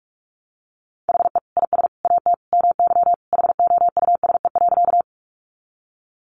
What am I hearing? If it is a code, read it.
Morse code, heard as a keyed tone, about 44 wpm, 700 Hz. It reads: HE IS AT MY HOUSE.
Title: Sentences from Top 200 Words 45wpm